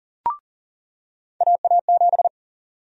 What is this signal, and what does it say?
Morse code, heard as a keyed tone, about 40 wpm, 700 Hz. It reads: AA7
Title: USA Call Sign Prefixes - 40wpm